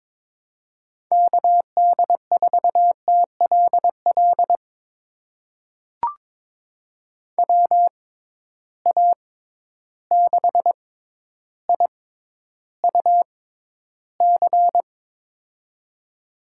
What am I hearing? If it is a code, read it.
Morse code, heard as a keyed tone, about 22 wpm, spaced as for 11 wpm, 700 Hz. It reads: KD4TLL WA6IUC
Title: USA Call Signs - 6 Characters - 6x Character Spacing - 22wpm